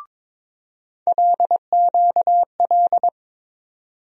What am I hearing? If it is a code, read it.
Morse code, heard as a keyed tone, about 22 wpm, 700 Hz. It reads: LQL
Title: USA Call Sign Postfixes - 3 Letters 22wpm